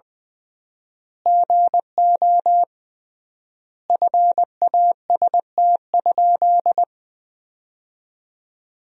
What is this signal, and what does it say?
Morse code, heard as a keyed tone, about 21 wpm, 700 Hz. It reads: GO FAST?
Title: 3x Word-Spacing - Sentences from Top 400 Words - 20wpm